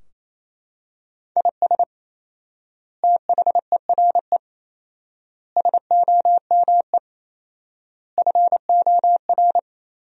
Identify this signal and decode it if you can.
Morse code, heard as a keyed tone, about 28 wpm, 700 Hz. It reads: IS THERE SOME FOR
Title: 4x Word-Spacing - Sentences from Top 200 Words - 28wpm